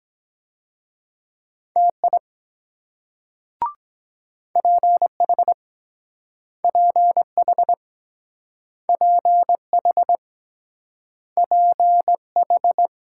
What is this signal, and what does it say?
Morse code, heard as a keyed tone, about 23 wpm, 700 Hz. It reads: TI PH PH PH PH